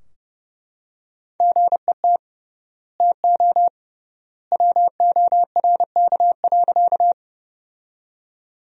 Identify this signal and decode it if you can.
Morse code, heard as a keyed tone, about 30 wpm, 700 Hz. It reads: GET TO WORK.
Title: 3x Word-Spacing - Sentences from Top 200 Words - 30wpm